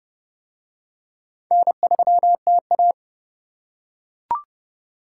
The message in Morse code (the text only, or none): N3TA